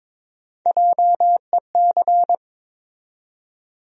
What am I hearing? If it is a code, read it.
Morse code, heard as a keyed tone, about 22 wpm, 700 Hz. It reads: JEC
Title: USA Call Sign Postfixes - 3 Letters 22wpm